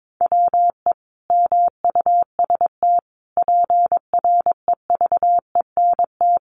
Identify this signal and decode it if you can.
Morse code, heard as a keyed tone, about 22 wpm, 700 Hz. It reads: WE MUST PREVENT
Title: Sentences from Top 1100 Words 22wpm